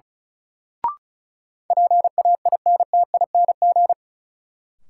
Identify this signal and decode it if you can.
Morse code, heard as a keyed tone, about 36 wpm, 700 Hz. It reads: PAINTING
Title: Sentences from Top 1200 Words - New Word Drill Alt - 35wpm